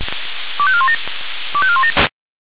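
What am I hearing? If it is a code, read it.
Five-tone selective call (ZVEI), heard as quick runs of five tones, tones of about 70 ms. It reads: 25417 25417
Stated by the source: ZVEI five-tone selective call 25202 received on an SDR.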